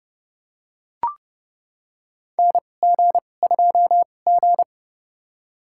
Morse code, heard as a keyed tone, about 32 wpm, spaced as for 22 wpm, 700 Hz. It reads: NG2G
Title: USA Call Signs - 4 Characters - 2x Character Spacing - 30wpm